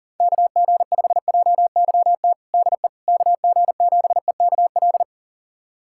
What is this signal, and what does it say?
Morse code, heard as a keyed tone, about 40 wpm, 700 Hz. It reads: KG5JYT DE KG7EKL
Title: USA Callsign DE Callsign 40wpm